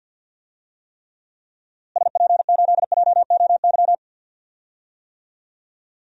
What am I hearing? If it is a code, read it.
Morse code, heard as a keyed tone, about 51 wpm, 700 Hz. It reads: SP8JOY
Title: DXCC-100 Call Signs - 6 Characters - Repeated 2x - 50wpm